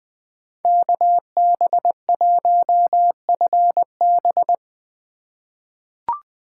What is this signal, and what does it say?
Morse code, heard as a keyed tone, about 20 wpm, 700 Hz. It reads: KB1FB
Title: USA Call Signs - 8x Character Spacing - 20wpm